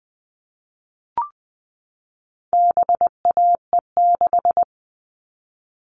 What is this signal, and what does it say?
Morse code, heard as a keyed tone, about 20 wpm, 700 Hz. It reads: BAE6
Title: All Cut Numbers - 4-Digit 20wpm